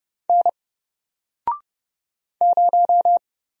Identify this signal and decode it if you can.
Morse code, heard as a keyed tone, about 30 wpm, 700 Hz. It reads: N 0
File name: Single-Letter-Number-30wpm